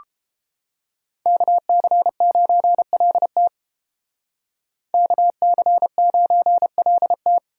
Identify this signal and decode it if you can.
Morse code, heard as a keyed tone, about 31 wpm, 700 Hz. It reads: KC9LT KC9LT